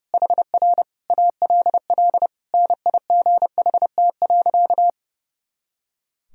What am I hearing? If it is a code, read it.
Morse code, heard as a keyed tone, about 30 wpm, 700 Hz. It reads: HR ALL NIGHT.
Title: Sentences from Top 300 Words and Abbreviations 30wpm